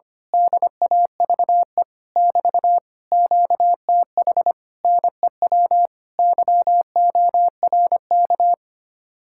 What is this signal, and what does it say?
Morse code, heard as a keyed tone, about 24 wpm, 700 Hz. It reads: DAVE <BT> QTH NEW YORK
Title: Speed-Racing Common QSO Phrases - 25wpm